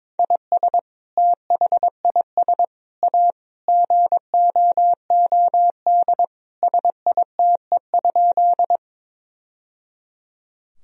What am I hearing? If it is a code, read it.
Morse code, heard as a keyed tone, about 21 wpm, 700 Hz. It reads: IS THIS A GOOD SITE?
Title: Sentences from Top 1000 Words - Review Part 4 - 22wpm